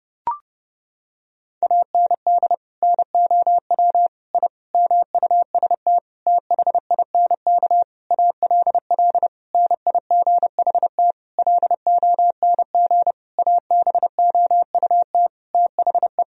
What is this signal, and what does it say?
Morse code, heard as a keyed tone, about 30 wpm, 700 Hz. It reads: AND NOW I MUST THINK ALL NIGHT LONG ABOUT THE
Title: Sentences from Top 1000 Words - Review Part 3 - 30wpm